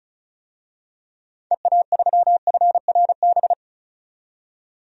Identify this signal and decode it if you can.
Morse code, heard as a keyed tone, about 34 wpm, 700 Hz. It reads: EA3FRB
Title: DXCC-100 Call Signs - 6 Characters - 35wpm